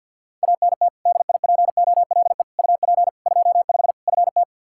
Morse code, heard as a keyed tone, about 52 wpm, 700 Hz. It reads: ANT DIPOLE UP 25 FT
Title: Practice QSO - Exchange 2 - Info - 50wpm